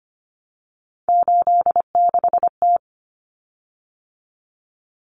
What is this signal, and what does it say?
Morse code, heard as a keyed tone, about 25 wpm, 700 Hz. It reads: 86T
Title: Common Cut Numbers - 3-Digit 25wpm